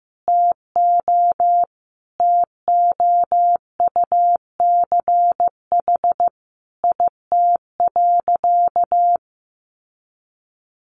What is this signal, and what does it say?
Morse code, heard as a keyed tone, about 15 wpm, 700 Hz. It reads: TO TOUCH IT.
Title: Sentences from Top 1200 Words 15wpm